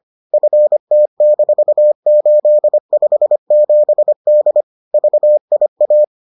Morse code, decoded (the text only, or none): FT-857D VIA